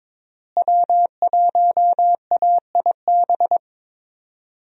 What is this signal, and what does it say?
Morse code, heard as a keyed tone, about 22 wpm, 700 Hz. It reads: W1AIB
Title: USA Call Signs - 5 Characters - Repeated 2x - 22wpm